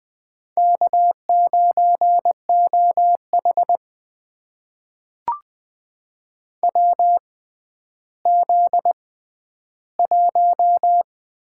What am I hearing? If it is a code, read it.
Morse code, heard as a keyed tone, about 21 wpm, spaced as for 12 wpm, 700 Hz. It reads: K9OH WZ1
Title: USA Call Signs - 4 Characters - 6x Character Spacing - 20wpm